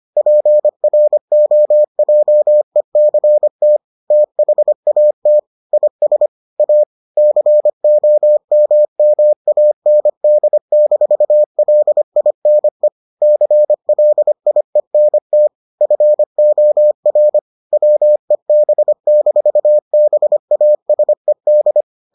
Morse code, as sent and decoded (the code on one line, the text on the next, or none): .--. .-. --- .--- . -.-. - / - .... .- - / .. ... / .- / -.-. --- -- -- .- -. -.. -....- .-.. .. -. / -.-. .-.. .. -. - / ..-. --- .-. / .-- . -... -....- -... .- ... -..
PROJECT THAT IS A COMMAND-LINE CLIENT FOR WEB-BASED